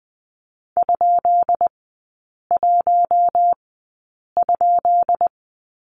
Morse code, heard as a keyed tone, about 20 wpm, spaced as for 8 wpm, 690 Hz. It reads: ?1?
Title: 198 Lesson 32e - Letter k - Three Characters 20wpm